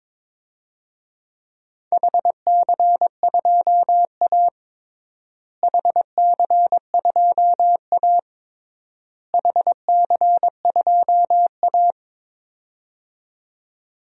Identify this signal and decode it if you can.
Morse code, heard as a keyed tone, about 22 wpm, 700 Hz. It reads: HC2A HC2A HC2A
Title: DXCC-100 Call Signs - 4 Characters - Repeated 3x - 22wpm